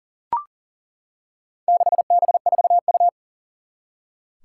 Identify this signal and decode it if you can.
Morse code, heard as a keyed tone, about 40 wpm, 690 Hz. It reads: 6B4U
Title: Four Challenging Letter-Number Combinations 40wpm